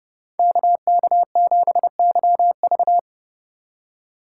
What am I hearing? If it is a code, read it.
Morse code, heard as a keyed tone, about 32 wpm, 700 Hz. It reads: KK7YV